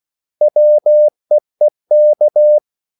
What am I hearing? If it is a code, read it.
Morse code, heard as a keyed tone, about 16 wpm, 600 Hz. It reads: WEEK
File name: CW / Practice / 500MostCommonEnglishWords16